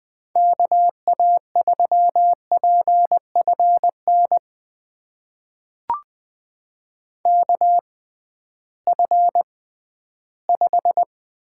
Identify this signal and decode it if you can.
Morse code, heard as a keyed tone, about 20 wpm, spaced as for 13 wpm, 700 Hz. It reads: KA3PFN KF5
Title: USA Call Signs - 6 Characters - 6x Character Spacing - 20wpm